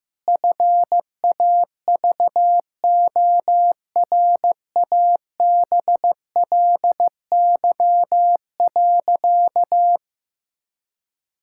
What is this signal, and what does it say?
Morse code, heard as a keyed tone, about 15 wpm, 700 Hz. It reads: FAVORABLY.
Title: Pangram Challenge v2 - 15wpm